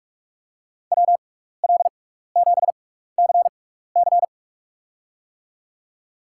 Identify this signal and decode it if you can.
Morse code, heard as a keyed tone, about 43 wpm, spaced as for 15 wpm, 700 Hz. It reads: WL7CC
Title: USA Call Signs - 5 Characters - 6x Character Spacing - 45wpm